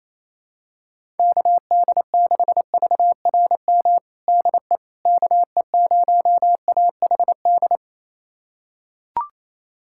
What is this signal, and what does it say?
Morse code, heard as a keyed tone, about 28 wpm, 700 Hz. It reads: KD6VRM DE KE0AHD
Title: USA Callsign DE Callsign 28wpm